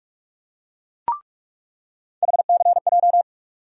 Morse code, heard as a keyed tone, about 46 wpm, 700 Hz. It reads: HCJ
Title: USA Call Sign Postfixes - 3 Letters 45wpm